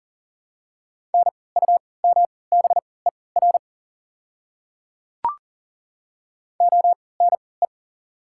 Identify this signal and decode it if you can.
Morse code, heard as a keyed tone, about 40 wpm, spaced as for 23 wpm, 690 Hz. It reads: NUMBER ONE
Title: 3x Character Spacing - Top 100 Words - 40wpm